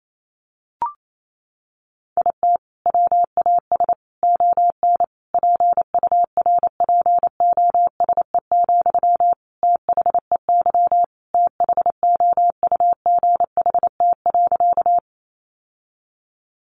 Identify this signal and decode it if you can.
Morse code, heard as a keyed tone, about 28 wpm, 700 Hz. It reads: IT WAS ON PURPOSE, THEY THOUGHT.
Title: Sentences from Top 800 Words 28wpm